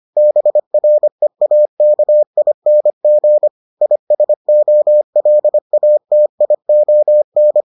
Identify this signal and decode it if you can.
Morse code, heard as a keyed tone, about 25 wpm, 600 Hz. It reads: BREAKING ISOLATION